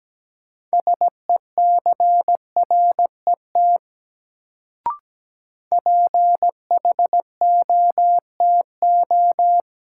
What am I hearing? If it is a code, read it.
Morse code, heard as a keyed tone, about 17 wpm, 700 Hz. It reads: SECRET PHOTO
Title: Sentences from Top 1200 Words - New Word Drill Alt - 17wpm